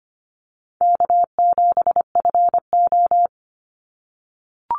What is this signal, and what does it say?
Morse code, heard as a keyed tone, about 25 wpm, 700 Hz. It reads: K7FO
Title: NA SOTA Chaser Callsigns 25wpm